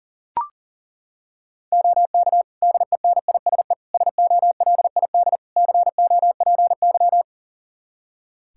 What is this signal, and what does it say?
Morse code, heard as a keyed tone, about 40 wpm, 700 Hz. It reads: OK DENISE SOLID COPY
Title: Practice QSO - Exchange 3 - More Info - 40wpm